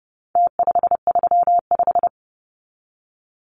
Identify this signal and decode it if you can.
Morse code, heard as a keyed tone, about 30 wpm, 700 Hz. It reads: T535
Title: Numbers - Contest Style - 4-Digit 30wpm